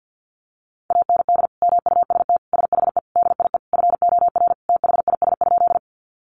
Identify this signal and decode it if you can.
Morse code, heard as a keyed tone, about 50 wpm, 700 Hz. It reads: AND MUST SHE DIE FOR THIS?
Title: Sentences from Top 1300 Words 50wpm